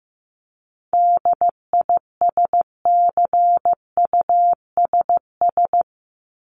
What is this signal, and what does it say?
Morse code, heard as a keyed tone, about 15 wpm, 700 Hz. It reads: DISCUSS
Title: ED Suffix - Top 100 Words with Base Words 15wpm